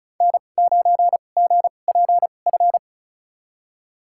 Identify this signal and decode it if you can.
Morse code, heard as a keyed tone, about 36 wpm, spaced as for 25 wpm, 700 Hz. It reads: N9GPF
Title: USA Call Signs - 5 Characters - 2x Character Spacing - 35wpm